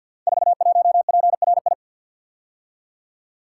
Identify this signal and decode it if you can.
Morse code, heard as a keyed tone, about 48 wpm, 700 Hz. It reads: 41PRI
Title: Code Groups - 5 Letter:Numbers - 50wpm